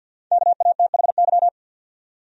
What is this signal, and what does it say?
Morse code, heard as a keyed tone, about 47 wpm, 700 Hz. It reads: KATHY
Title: USA Common Names - 50wpm